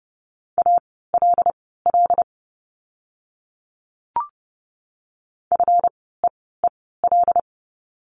Morse code, heard as a keyed tone, about 30 wpm, spaced as for 17 wpm, 700 Hz. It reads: ALL FEEL
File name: course / 30:10wpm / 230 Lesson 37f - Letter x - Words 30wpm